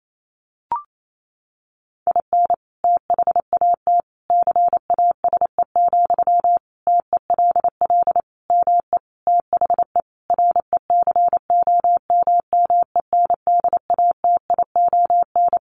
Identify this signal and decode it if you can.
Morse code, heard as a keyed tone, about 28 wpm, 700 Hz. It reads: IN THAT CASE, TELL ME THE RECOMMENDATION
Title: Sentences from Top 1400 Words 28wpm